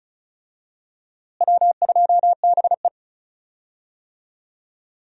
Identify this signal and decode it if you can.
Morse code, heard as a keyed tone, about 34 wpm, 700 Hz. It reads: W2BE
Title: USA Call Signs - 4 Characters - Repeated 2x - 35wpm